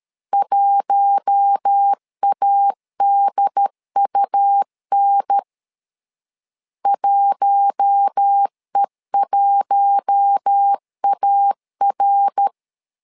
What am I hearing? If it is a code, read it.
Morse code, heard as a keyed tone, about 13 wpm, 800 Hz. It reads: 1ADUN 1E1AR
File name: LEZIONI / 4 Q1 12wpm 2028